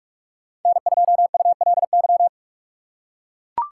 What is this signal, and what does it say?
Morse code, heard as a keyed tone, about 43 wpm, 700 Hz. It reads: N2ULY
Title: USA Call Signs - 5 Characters - 2x Character Spacing - 45wpm